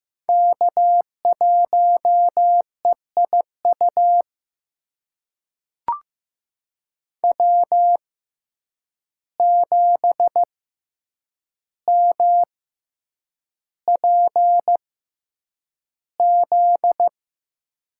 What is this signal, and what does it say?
Morse code, heard as a keyed tone, about 15 wpm, spaced as for 8 wpm, 700 Hz. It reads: K1EIU W7MPZ
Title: USA Call Signs - 5 Characters - 6x Character Spacing - 15wpm